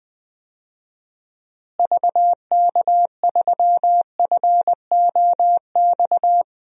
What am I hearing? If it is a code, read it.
Morse code, heard as a keyed tone, about 21 wpm, 700 Hz. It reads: VK3FOX
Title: DXCC-100 Call Signs - 6 Characters - Repeated 3x - 20wpm